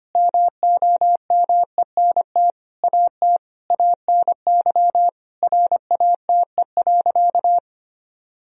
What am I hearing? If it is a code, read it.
Morse code, heard as a keyed tone, about 25 wpm, 700 Hz. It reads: MOMENT AT ANY RATE.
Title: Sentences from Top 1000 Words - Review Part 4 - 25wpm